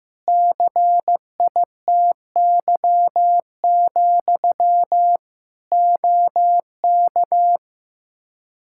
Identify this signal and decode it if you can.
Morse code, heard as a keyed tone, about 16 wpm, 700 Hz. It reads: CITY, OK
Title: Practice QSO - Exchange 1 - Beginning - 15wpm